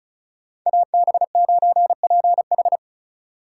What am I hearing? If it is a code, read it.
Morse code, heard as a keyed tone, about 47 wpm, spaced as for 41 wpm, 700 Hz. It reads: AB9PH
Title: Speed Racing - USA 2x2 Call Signs 35wpm